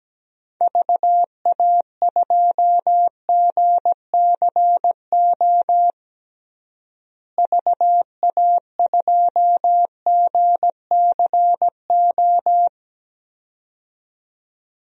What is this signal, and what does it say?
Morse code, heard as a keyed tone, about 17 wpm, 700 Hz. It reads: VA2GCO VA2GCO
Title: DXCC-100 Call Signs - 6 Characters - Repeated 2x - 17wpm